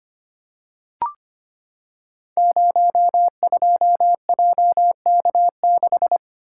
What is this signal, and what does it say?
Morse code, heard as a keyed tone, about 25 wpm, 700 Hz. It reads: 02JK6
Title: Code Groups - 5 Letter:Numbers - 25wpm